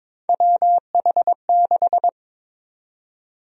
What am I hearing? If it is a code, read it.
Morse code, heard as a keyed tone, about 21 wpm, 700 Hz. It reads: WH6
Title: USA Call Sign Prefixes - 22wpm